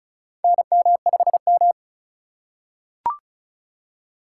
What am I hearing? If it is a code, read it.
Morse code, heard as a keyed tone, about 36 wpm, 700 Hz. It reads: NM5M
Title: USA Call Signs - 4 Characters - 4x Character Spacing - 35wpm